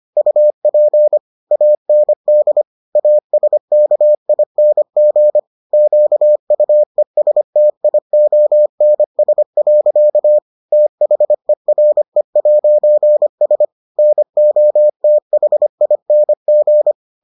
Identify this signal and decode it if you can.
Morse code, heard as a keyed tone, about 25 wpm, 600 Hz. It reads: UP AND ASKING QUESTIONS. THERE'S NOTHING